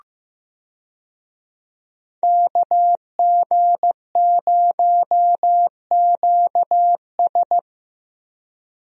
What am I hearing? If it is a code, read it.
Morse code, heard as a keyed tone, about 15 wpm, 700 Hz. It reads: KG0QS
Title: USA Call Signs - 5 Characters - Repeated 2x - 15wpm